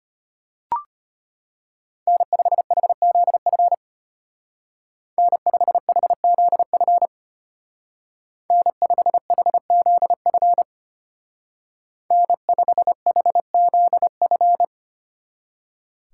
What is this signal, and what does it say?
Morse code, heard as a keyed tone, about 32 wpm, 700 Hz. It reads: N5HZF N5HZF N5HZF N5HZF